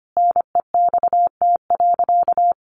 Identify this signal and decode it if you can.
Morse code, heard as a keyed tone, about 25 wpm, 700 Hz. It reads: NEXT.
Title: Sentences from Top 1500 Words 25wpm